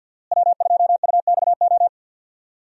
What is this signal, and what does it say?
Morse code, heard as a keyed tone, about 46 wpm, 700 Hz. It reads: W2UXO
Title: USA Call Signs - 5 Characters - Repeated 2x - 50wpm